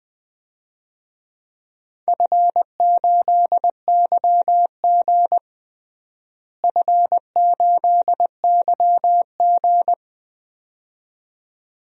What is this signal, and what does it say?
Morse code, heard as a keyed tone, about 21 wpm, 700 Hz. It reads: F8YG F8YG
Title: DXCC-100 Call Signs - 4 Characters - Repeated 2x - 20wpm